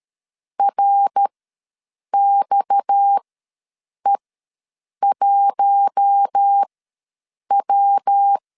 Morse code, heard as a keyed tone, about 13 wpm, spaced as for 7 wpm, 800 Hz. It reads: RXE1W
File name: LEZIONI / 6 PKX0 / 6 PKX ws 12wpm 3106